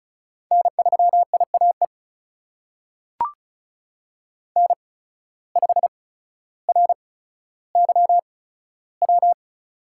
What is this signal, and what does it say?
Morse code, heard as a keyed tone, about 36 wpm, spaced as for 15 wpm, 700 Hz. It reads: N3IAE N5RYW
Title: USA Call Signs - 5 Characters - 8x Character Spacing - 35wpm